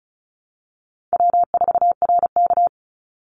Morse code, heard as a keyed tone, about 34 wpm, 700 Hz. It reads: W4RK
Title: Call Signs Repeated 2x - 35wpm